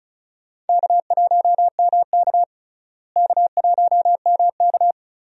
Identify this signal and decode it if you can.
Morse code, heard as a keyed tone, about 34 wpm, 700 Hz. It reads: K1MK K1MK
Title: USA Call Signs - 4 Characters - Repeated 3x - 35wpm